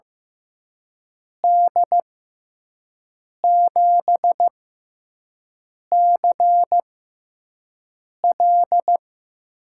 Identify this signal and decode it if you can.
Morse code, heard as a keyed tone, about 15 wpm, spaced as for 5 wpm, 700 Hz. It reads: D7CL
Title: USA Call Signs - 6x Character Spacing - 15wpm